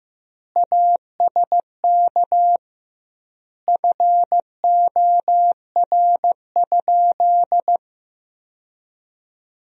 Morse code, heard as a keyed tone, about 15 wpm, 700 Hz. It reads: ASK FOR?
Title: 2x Word-Spacing - Sentences from Top 200 Words - 15wpm